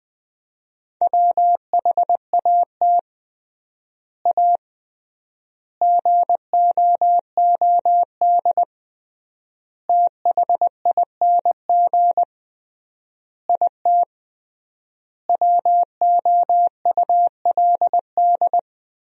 Morse code, heard as a keyed tone, about 19 wpm, 700 Hz. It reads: WHAT A GOOD THING IT WOULD
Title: 3x Word-Spacing - Sentences from Top 200 Words - 20wpm